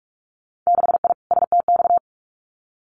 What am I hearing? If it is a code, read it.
Morse code, heard as a keyed tone, about 46 wpm, 700 Hz. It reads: 6I STX
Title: Winter Field Day Exchanges Encore 45wpm